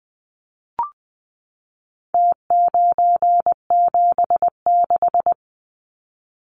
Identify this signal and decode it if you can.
Morse code, heard as a keyed tone, about 20 wpm, 700 Hz. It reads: T976
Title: Numbers - Contest Style - 4-Digit 20wpm